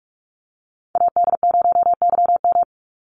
Morse code, heard as a keyed tone, about 46 wpm, 700 Hz. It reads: AD0YM
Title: NA SOTA Chaser Callsigns 45wpm